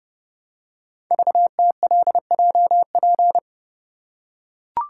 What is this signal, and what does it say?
Morse code, heard as a keyed tone, about 30 wpm, 700 Hz. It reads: VTLJP
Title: Code Groups - 5 Letters - 30wpm